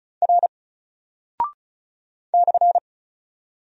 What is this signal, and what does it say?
Morse code, heard as a keyed tone, about 35 wpm, 700 Hz. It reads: R /
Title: Single Character (Letter:Number:Punctuation:Prosign) 35wpm